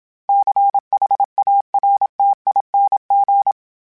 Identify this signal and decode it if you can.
Morse code, heard as a keyed tone, about 27 wpm, 800 Hz. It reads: CHARTING
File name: Special INGs-mp3s / Special INGs-mp3s / Sngl Ing @25